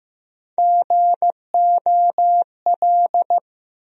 Morse code, heard as a keyed tone, about 15 wpm, 700 Hz. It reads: GOL
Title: USA Call Sign Postfixes - 3 Letters 15wpm